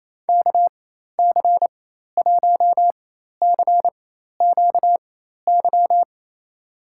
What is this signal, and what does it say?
Morse code, heard as a keyed tone, about 28 wpm, spaced as for 13 wpm, 700 Hz. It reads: KC1CQY
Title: USA Call Signs - 6 Characters - 4x Character Spacing - 28wpm